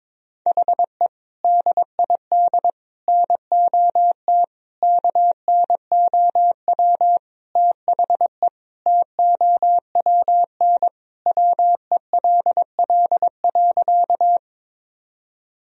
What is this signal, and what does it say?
Morse code, heard as a keyed tone, about 22 wpm, 700 Hz. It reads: HE DID NOT KNOW THE TOWN WELL.
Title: Sentences from Top 1000 Words - Review Part 2 - 22wpm